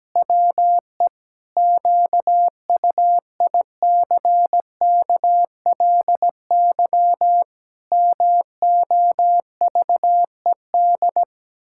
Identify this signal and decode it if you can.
Morse code, heard as a keyed tone, about 17 wpm, 700 Hz. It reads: WE QUICKLY MOVED
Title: Pangram Challenge v2 - 17wpm